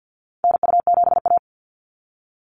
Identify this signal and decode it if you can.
Morse code, heard as a keyed tone, about 48 wpm, 700 Hz. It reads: NU7A
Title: NA SOTA Activator Callsigns 50wpm